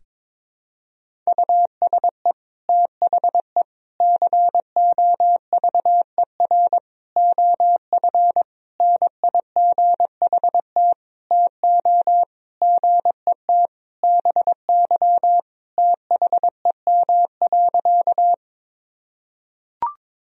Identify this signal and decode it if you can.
Morse code, heard as a keyed tone, about 22 wpm, 700 Hz. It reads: USE THE COVER OF NIGHT TO GET BY THEM.
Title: Sentences from Top 1000 Words - Review Part 1 - 22wpm